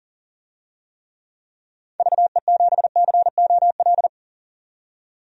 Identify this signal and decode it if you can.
Morse code, heard as a keyed tone, about 40 wpm, 700 Hz. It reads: VE7COL
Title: DXCC-100 Call Signs - 6 Characters - 40wpm